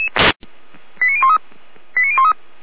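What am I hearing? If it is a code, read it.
Five-tone selective call (ZVEI), heard as quick runs of five tones, tones of about 70 ms. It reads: 89013 89013
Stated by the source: ZVEI five-tone selective call 89337 received by SDR, from the Feuerwehr Orlamünde (fire brigade) audio server.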